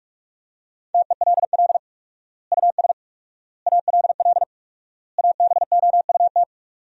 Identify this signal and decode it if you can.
Morse code, heard as a keyed tone, about 42 wpm, 700 Hz. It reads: TELL US ALL ABOUT
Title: 4x Word-Spacing - Sentences from Top 200 Words - 45wpm